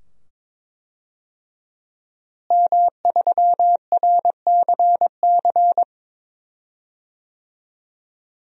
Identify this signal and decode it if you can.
Morse code, heard as a keyed tone, about 21 wpm, 700 Hz. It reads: M3RCC
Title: DXCC-100 Call Signs - 5 Characters - Repeated 2x - 22wpm